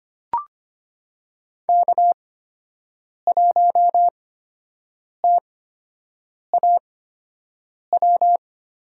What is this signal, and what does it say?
Morse code, heard as a keyed tone, about 25 wpm, spaced as for 7 wpm, 700 Hz. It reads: K1TAW
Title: USA Call Signs - 5 Characters - 8x Character Spacing - 25wpm